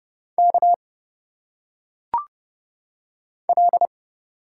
Morse code, heard as a keyed tone, about 29 wpm, 700 Hz. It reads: K L
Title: Single Letters - Phonetic Pronunciation 30wpm